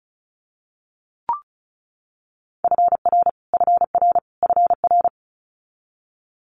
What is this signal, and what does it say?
Morse code, heard as a keyed tone, about 36 wpm, 700 Hz. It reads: FR FR FR
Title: Multi-letter Phonemes Repeated 3x 35wpm